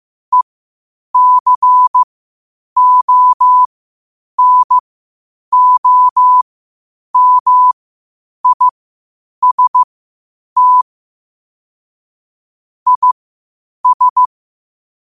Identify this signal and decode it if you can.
Morse code, heard as a keyed tone, about 15 wpm, spaced as for 8 wpm, 1000 Hz. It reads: ECONOMIST IS
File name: qotdfiles / 8wpm